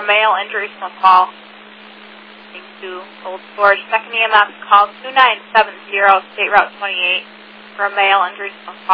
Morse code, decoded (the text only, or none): none